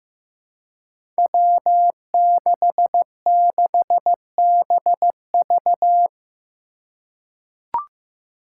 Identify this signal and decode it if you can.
Morse code, heard as a keyed tone, about 15 wpm, 700 Hz. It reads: W66BV
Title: Five Challenging Letter-Number Combinations 15wpm